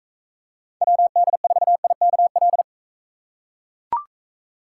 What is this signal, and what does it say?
Morse code, heard as a keyed tone, about 42 wpm, 700 Hz. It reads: WD4IKL